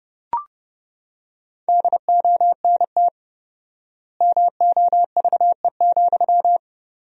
Morse code, heard as a keyed tone, about 30 wpm, 700 Hz. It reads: DONT MOVE,
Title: 4x Word-Spacing - Sentences from Top 300 Words - 30wpm